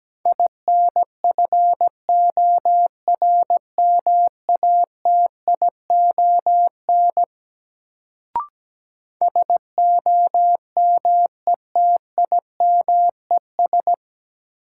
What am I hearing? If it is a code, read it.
Morse code, heard as a keyed tone, about 17 wpm, 700 Hz. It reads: INFORMATION SOMETIMES
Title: Sentences from Top 600 Words - New Word Drill - 17wpm